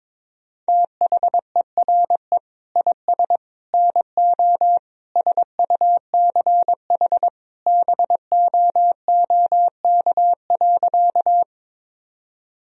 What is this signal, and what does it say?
Morse code, heard as a keyed tone, about 22 wpm, 700 Hz. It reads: THERE IS NO SUCH BOOK.
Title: Sentences from Top 1000 Words - Review Part 1 - 22wpm